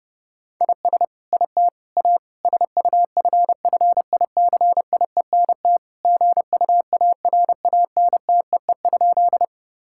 Morse code, heard as a keyed tone, about 30 wpm, 700 Hz. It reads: IS IT A SUFFICIENT GUARANTEE?
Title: Sentences from Top 1600 Words 30wpm